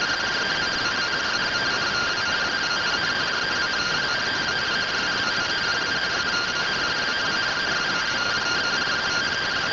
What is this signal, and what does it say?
Two-tone teleprinter signal, about 170 Hz shift.
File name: findings / CT AXIAL -111.gen-33 rtty rx